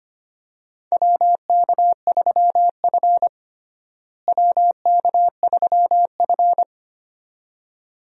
Morse code, heard as a keyed tone, about 25 wpm, 700 Hz. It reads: WK3F WK3F
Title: USA Call Signs - 4 Characters - Repeated 2x - 25wpm